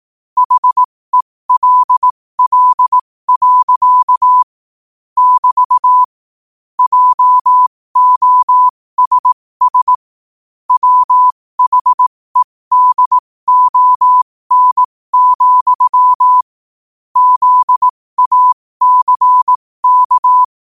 Morse code, heard as a keyed tone, about 18 wpm, spaced as for 15 wpm, 1000 Hz. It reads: HELL. <BT> JOSS WHEDON, ZACK
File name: qotdfiles / 15wpm